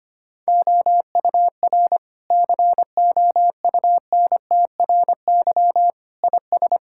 Morse code, heard as a keyed tone, about 25 wpm, 700 Hz. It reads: OUR COUNTRY IS
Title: Sentences from Top 1000 Words - Review Part 6 - 25wpm